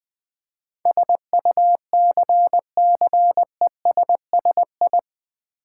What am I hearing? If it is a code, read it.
Morse code, heard as a keyed tone, about 20 wpm, 700 Hz. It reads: SUCCESSI
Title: Build A Word - 4 Letters At A Time - 10-Letter Words - 20wpm